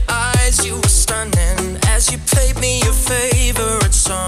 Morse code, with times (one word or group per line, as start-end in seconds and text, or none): none